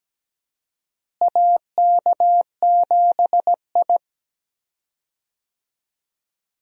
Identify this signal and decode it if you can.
Morse code, heard as a keyed tone, about 17 wpm, 700 Hz. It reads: AK7I
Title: USA Call Signs - 4 Characters - Repeated 2x - 17wpm